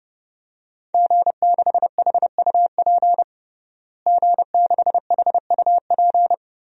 Morse code, heard as a keyed tone, about 30 wpm, 700 Hz. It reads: G6HUP G6HUP
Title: DXCC-100 Call Signs - 5 Characters - Repeated 2x - 30wpm